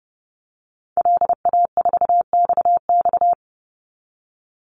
Morse code, heard as a keyed tone, about 30 wpm, 700 Hz. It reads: LA4XX
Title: Call Signs Repeated 3x - 30wpm